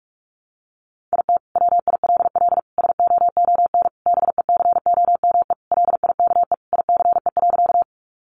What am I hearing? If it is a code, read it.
Morse code, heard as a keyed tone, about 44 wpm, 700 Hz. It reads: IT WILL SOON BECOME LIKE ICE.
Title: Sentences from Top 1000 Words 45wpm